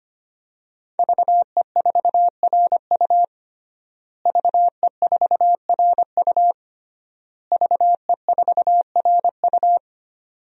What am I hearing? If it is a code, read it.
Morse code, heard as a keyed tone, about 25 wpm, 700 Hz. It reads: VE4RU VE4RU VE4RU
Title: DXCC-100 Call Signs - 5 Characters - Repeated 3x - 25wpm